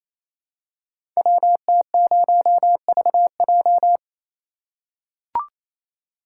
Morse code, heard as a keyed tone, about 28 wpm, 700 Hz. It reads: WT0VJ
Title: Code Groups - 5 Letter:Numbers - 28wpm